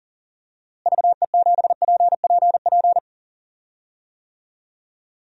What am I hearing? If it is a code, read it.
Morse code, heard as a keyed tone, about 40 wpm, 700 Hz. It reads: VE7PPP
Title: DXCC-100 Call Signs - 6 Characters - Repeated 3x - 40wpm